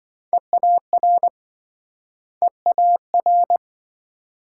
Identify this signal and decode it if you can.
Morse code, heard as a keyed tone, about 28 wpm, 700 Hz. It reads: EAR EAR